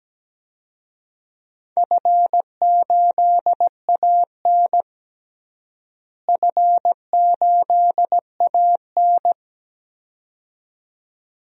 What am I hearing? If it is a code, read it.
Morse code, heard as a keyed tone, about 17 wpm, 700 Hz. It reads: F8AN F8AN
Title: DXCC-100 Call Signs - 4 Characters - Repeated 2x - 17wpm